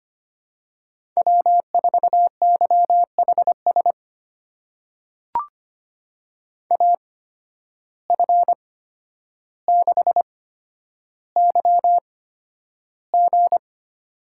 Morse code, heard as a keyed tone, about 25 wpm, spaced as for 11 wpm, 700 Hz. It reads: W4YHS AF6YG
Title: USA Call Signs - 5 Characters - 8x Character Spacing - 25wpm